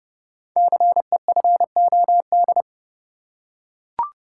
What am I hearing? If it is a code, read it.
Morse code, heard as a keyed tone, about 29 wpm, 700 Hz. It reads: CEFOD
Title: Code Groups - 5 Letters - 30wpm